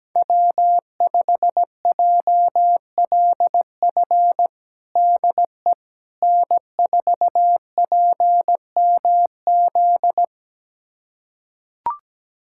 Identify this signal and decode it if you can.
Morse code, heard as a keyed tone, about 17 wpm, 700 Hz. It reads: W5JLF DE N4PMZ
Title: USA Callsign DE Callsign - 5 Characters - 17wpm